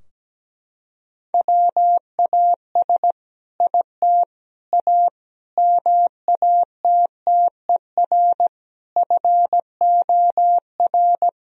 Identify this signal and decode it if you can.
Morse code, heard as a keyed tone, about 17 wpm, 700 Hz. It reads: WAS IT A MATTER FOR
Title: Sentences from Top 1000 Words - Review Part 4 - 17wpm